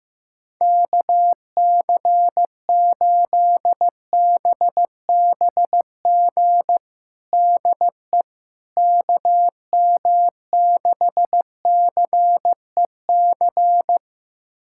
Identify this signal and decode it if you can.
Morse code, heard as a keyed tone, about 15 wpm, 700 Hz. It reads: KC8BBG DE KM6CEC
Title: USA Callsign DE Callsign 15wpm